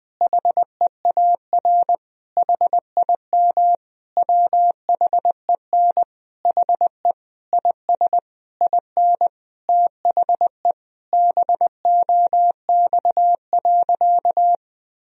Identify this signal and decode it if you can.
Morse code, heard as a keyed tone, about 20 wpm, 700 Hz. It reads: HEAR HIM WHEN HE IS IN THE BOX.
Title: Sentences from Top 1000 Words - Review Part 2 - 20wpm